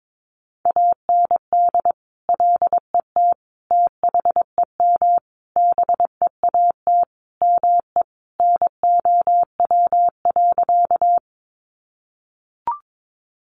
Morse code, heard as a keyed tone, about 22 wpm, 700 Hz. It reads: AND LET THEM BEAT ME NOW.
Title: Sentences from Top 1300 Words 22wpm